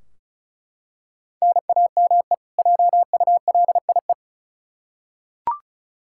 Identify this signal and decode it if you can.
Morse code, heard as a keyed tone, about 34 wpm, 700 Hz. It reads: NAME JULIE
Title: K1USN SST QSO Practice 35wpm